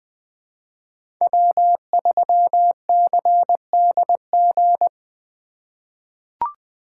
Morse code, heard as a keyed tone, about 20 wpm, 700 Hz. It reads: W3CDG
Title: USA Call Signs - 5 Characters - 3x Character Spacing - 20wpm